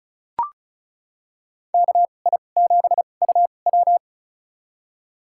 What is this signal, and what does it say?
Morse code, heard as a keyed tone, about 34 wpm, spaced as for 25 wpm, 700 Hz. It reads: KI7UW
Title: USA Call Signs - 2x Character Spacing - 35wpm